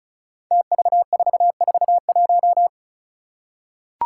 Morse code, heard as a keyed tone, about 36 wpm, 700 Hz. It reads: TV441
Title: Five Challenging Letter-Number Combinations 35wpm